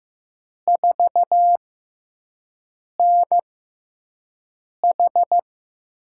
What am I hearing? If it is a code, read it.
Morse code, heard as a keyed tone, about 15 wpm, spaced as for 5 wpm, 700 Hz. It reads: 4NH
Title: USA Call Signs - 4 Characters - 6x Character Spacing - 15wpm